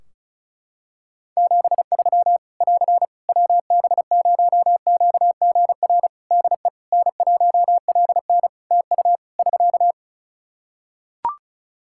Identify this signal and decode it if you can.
Morse code, heard as a keyed tone, about 36 wpm, 700 Hz. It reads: 73 <AR> WB0QGR DE N1LN TU <SK>
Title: Practice QSO - Exchange 4 - Ending - 35wpm